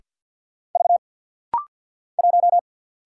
Morse code, heard as a keyed tone, about 50 wpm, 690 Hz. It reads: V 1
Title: Single Letter-Number Alt - Rapid-Fire 50wpm